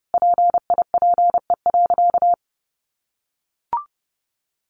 Morse code, heard as a keyed tone, about 30 wpm, 700 Hz. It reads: PIPE.
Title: Sentences from Top 1500 Words 30wpm